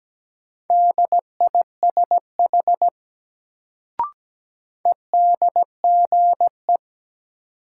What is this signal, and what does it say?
Morse code, heard as a keyed tone, about 17 wpm, 700 Hz. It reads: DISH EDGE
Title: Sentences from Top 1300 Words - New Word Drill - 17wpm